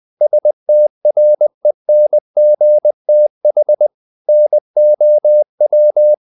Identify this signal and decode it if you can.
Morse code, heard as a keyed tone, about 20 wpm, 600 Hz. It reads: STRENGTH NOW
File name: the old man and the sea part 2 20wpm